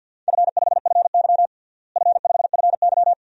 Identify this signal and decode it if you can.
Morse code, heard as a keyed tone, about 46 wpm, 700 Hz. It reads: F5FY F5FY
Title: DXCC-100 Call Signs - 4 Characters - Repeated 2x - 50wpm